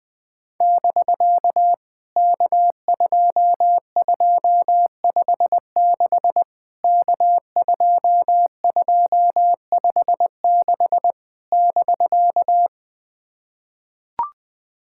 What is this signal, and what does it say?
Morse code, heard as a keyed tone, about 20 wpm, 700 Hz. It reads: <BK> K2256 K2256 <BK>
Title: POTA QSO Practice - 20wpm